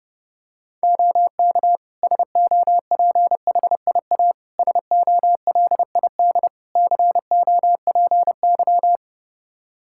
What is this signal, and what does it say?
Morse code, heard as a keyed tone, about 30 wpm, 700 Hz. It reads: OK SOPHIA SOLID COPY
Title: Practice QSO - Exchange 3 - More Info - 30wpm